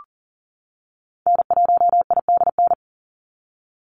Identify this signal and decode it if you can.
Morse code, heard as a keyed tone, about 40 wpm, 700 Hz. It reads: N1IDN
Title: NA SOTA Activator Callsigns 40wpm